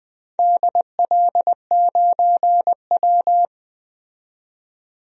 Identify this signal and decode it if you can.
Morse code, heard as a keyed tone, about 20 wpm, 700 Hz. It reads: DL9W
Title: DXCC-100 Call Signs - 4 Characters - 20wpm